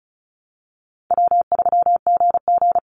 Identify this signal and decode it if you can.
Morse code, heard as a keyed tone, about 34 wpm, 700 Hz. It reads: W3GG
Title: Call Signs - 35wpm